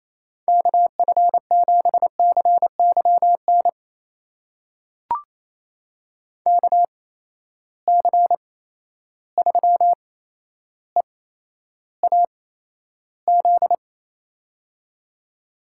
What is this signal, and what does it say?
Morse code, heard as a keyed tone, about 28 wpm, spaced as for 12 wpm, 700 Hz. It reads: KF7CYN KC3EAZ
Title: USA Call Signs - 6 Characters - 8x Character Spacing - 28wpm